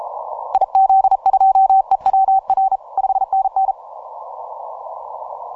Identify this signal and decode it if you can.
Morse code, heard as a keyed tone, about 39 wpm, 770 Hz. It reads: IZ2EWR 5NN
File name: simple-file-list / RI0B AS121 20MCW